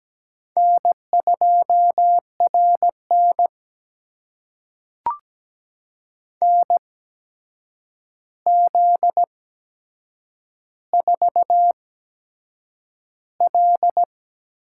Morse code, heard as a keyed tone, about 17 wpm, spaced as for 7 wpm, 700 Hz. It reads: N2RN NZ4L
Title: USA Call Signs - 8x Character Spacing - 17wpm